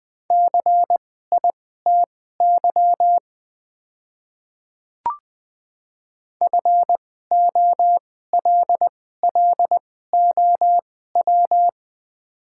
Morse code, heard as a keyed tone, about 21 wpm, spaced as for 15 wpm, 700 Hz. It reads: CITY FOLLOW
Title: 2x Character Spacing - Top 300 Words - 20wpm